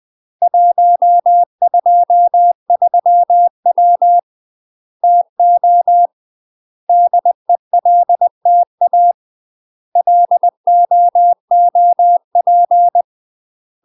Morse code, heard as a keyed tone, about 20 wpm, 700 Hz. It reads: 123W TO DELTA LOOP
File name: UK MASTER QSOs TRAINING / 20WPM / UK MASTER QSOs 0026